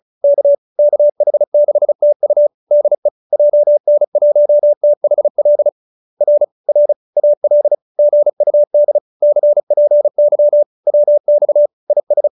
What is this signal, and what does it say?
Morse code, heard as a keyed tone, about 34 wpm, 590 Hz. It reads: K KH6TU DE JN1THL R R AL GUD CPY WX IS